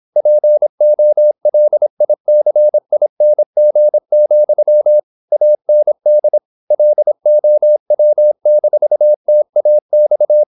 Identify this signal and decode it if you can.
Morse code, heard as a keyed tone, about 26 wpm, 600 Hz. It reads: POLICING, AND LOW-TAX
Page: Never 26wpm | CW med Gnister